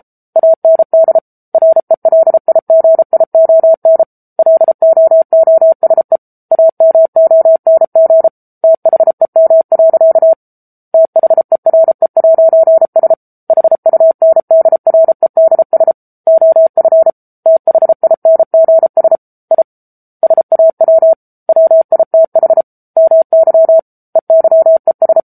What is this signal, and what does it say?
Morse code, heard as a keyed tone, about 34 wpm, 650 Hz. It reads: AND RELIGION LOOSE AMONG THEM. THERE'S HUNDREDS OF THINGS I SAW WITH MY EYES
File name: Book Two / 07-Chapter 7 - 33 WPM